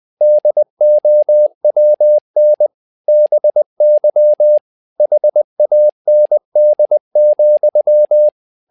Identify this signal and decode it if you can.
Morse code, heard as a keyed tone, about 20 wpm, 600 Hz. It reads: DOWN BY HAND,
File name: tower trouble 20wpm